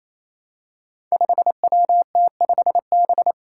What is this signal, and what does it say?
Morse code, heard as a keyed tone, about 27 wpm, 700 Hz. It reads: 5WT5B
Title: Five Challenging Letter-Number Combinations 28wpm